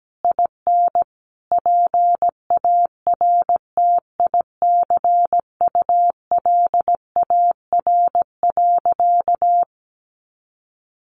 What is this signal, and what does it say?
Morse code, heard as a keyed tone, about 17 wpm, 700 Hz. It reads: IN PARTICULAR.
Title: Sentences from Top 900 Words 17wpm